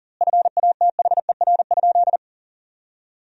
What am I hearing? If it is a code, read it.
Morse code, heard as a keyed tone, about 40 wpm, 700 Hz. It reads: FATHER?
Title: 2x Word-Spacing - Sentences from Top 400 Words - 40wpm